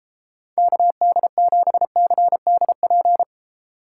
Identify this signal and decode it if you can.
Morse code, heard as a keyed tone, about 33 wpm, 700 Hz. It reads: KD7CDP